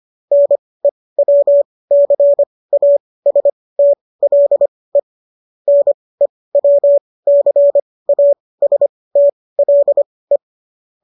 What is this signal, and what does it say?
Morse code, heard as a keyed tone, about 25 wpm, spaced as for 18 wpm, 570 Hz. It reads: NEWCASTLE NEWCASTLE
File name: qso207 18